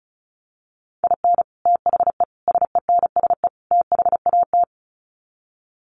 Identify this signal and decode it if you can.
Morse code, heard as a keyed tone, about 34 wpm, 700 Hz. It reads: IN THE SENSE THAT
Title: Sets of 4 Words 35wpm